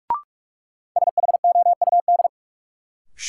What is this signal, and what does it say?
Morse code, heard as a keyed tone, about 44 wpm, 700 Hz. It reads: SHOUD
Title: Sentences from Top 1000 Words - Review Part 4 - New Word Drill - 45wpm